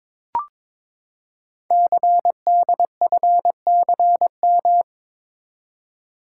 Morse code, heard as a keyed tone, about 22 wpm, 700 Hz. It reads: CDFCM
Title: Code Groups - 5 Letters - 22wpm